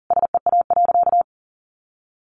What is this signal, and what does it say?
Morse code, heard as a keyed tone, about 40 wpm, 700 Hz. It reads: SEA.
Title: Sentences from Top 1100 Words 40wpm